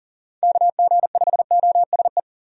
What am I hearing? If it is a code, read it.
Morse code, heard as a keyed tone, about 40 wpm, 700 Hz. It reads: KG5OSE
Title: USA Call Signs - 6 Characters - Repeated 2x - 40wpm